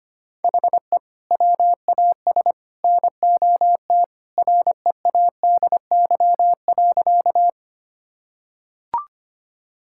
Morse code, heard as a keyed tone, about 25 wpm, 700 Hz. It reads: HE WAS NOT READY.
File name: practice / 25wpm / Sentences with 3-Word Phrases - 25wpm